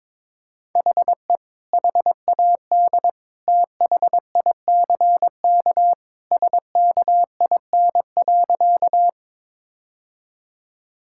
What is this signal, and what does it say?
Morse code, heard as a keyed tone, about 22 wpm, 700 Hz. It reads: HE HAD THICK SKIN.
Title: Sentences from Top 1600 Words 22wpm